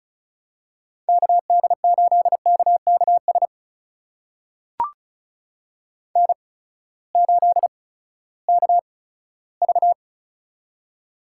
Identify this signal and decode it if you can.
Morse code, heard as a keyed tone, about 36 wpm, spaced as for 18 wpm, 700 Hz. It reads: KD8KKS N8KV
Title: USA Call Signs - 8x Character Spacing - 35wpm